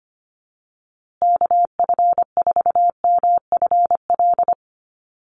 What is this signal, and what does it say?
Morse code, heard as a keyed tone, about 25 wpm, 700 Hz. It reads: KF4MFL
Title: USA Call Signs Repeated 2x 25wpm